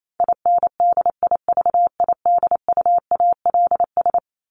Morse code, heard as a keyed tone, about 28 wpm, 700 Hz. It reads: INDIVIDUALS
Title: ES:S Suffix - Top 100 Words with Base Words 28wpm